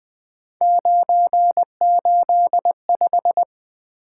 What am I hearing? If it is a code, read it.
Morse code, heard as a keyed tone, about 20 wpm, 700 Hz. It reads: 985
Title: Three Letter-Number Combinations - 20wpm